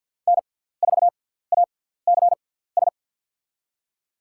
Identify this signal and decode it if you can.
Morse code, heard as a keyed tone, about 52 wpm, spaced as for 17 wpm, 700 Hz. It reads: N4ACS
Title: USA Call Signs - 5 Characters - 6x Character Spacing - 50wpm